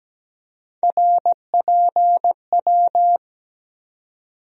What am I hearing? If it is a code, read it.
Morse code, heard as a keyed tone, about 17 wpm, 700 Hz. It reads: RPW
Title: USA Call Sign Postfixes - 3 Letters 17wpm